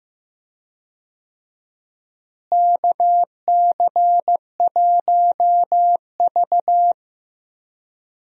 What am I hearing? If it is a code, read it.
Morse code, heard as a keyed tone, about 15 wpm, 700 Hz. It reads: KC1V
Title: USA Call Signs - 4 Characters - Repeated 2x - 15wpm